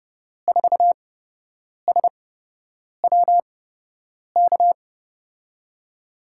Morse code, heard as a keyed tone, about 30 wpm, spaced as for 8 wpm, 700 Hz. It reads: 4SWK
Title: USA Call Signs - 5 Characters - 8x Character Spacing - 30wpm